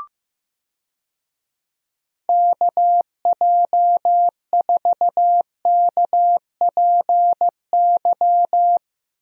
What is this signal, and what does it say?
Morse code, heard as a keyed tone, about 15 wpm, 700 Hz. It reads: KJ4KPY
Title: USA Call Signs - 6 Characters - Repeated 2x - 15wpm